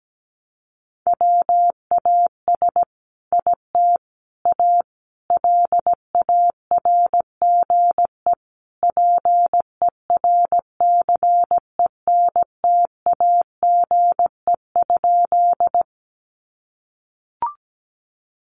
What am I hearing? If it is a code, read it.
Morse code, heard as a keyed tone, about 17 wpm, 700 Hz. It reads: WAS IT A LARGE PERCENTAGE?
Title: Sentences from Top 1300 Words 17wpm